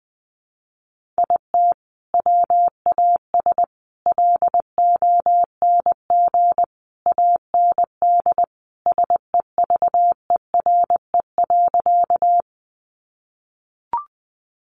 Morse code, heard as a keyed tone, about 20 wpm, 700 Hz. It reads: IT WAS LONG AND SEVERE.
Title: Sentences from Top 1400 Words 20wpm